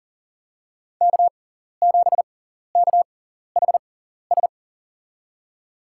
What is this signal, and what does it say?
Morse code, heard as a keyed tone, about 40 wpm, spaced as for 14 wpm, 700 Hz. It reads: K7KHS
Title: USA Call Signs - 5 Characters - 6x Character Spacing - 40wpm